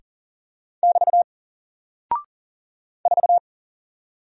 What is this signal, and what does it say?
Morse code, heard as a keyed tone, about 39 wpm, 700 Hz. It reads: <BT> 4
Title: Single Character (Letter:Number:Punctuation:Prosign) Alt 40wpm